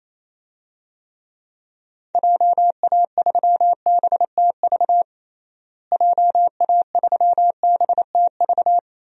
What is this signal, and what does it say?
Morse code, heard as a keyed tone, about 28 wpm, 700 Hz. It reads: JA3BTV JA3BTV
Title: DXCC-100 Call Signs - 6 Characters - Repeated 3x - 28wpm